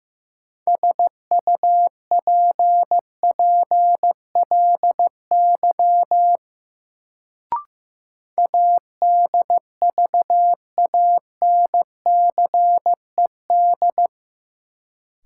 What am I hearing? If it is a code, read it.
Morse code, heard as a keyed tone, about 15 wpm, 700 Hz. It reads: SUPPLY ADVANCED
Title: Sentences from Top 1200 Words - New Word Drill Alt - 15wpm